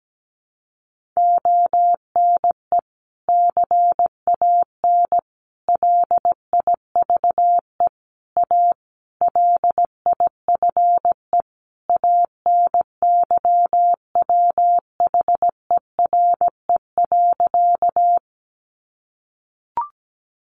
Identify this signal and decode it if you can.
Morse code, heard as a keyed tone, about 17 wpm, 700 Hz. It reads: ONE CAN LIVE A LIFE ANYWHERE.
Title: Sentences from Top 1400 Words 17wpm